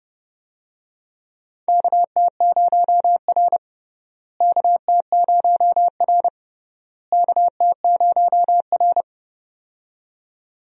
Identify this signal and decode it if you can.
Morse code, heard as a keyed tone, about 29 wpm, 700 Hz. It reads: KT0R KT0R KT0R
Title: USA Call Signs - 4 Characters - Repeated 3x - 30wpm